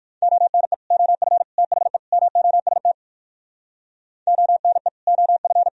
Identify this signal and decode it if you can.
Morse code, heard as a keyed tone, about 61 wpm, 700 Hz. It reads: ONE OF THE MOST ONE OF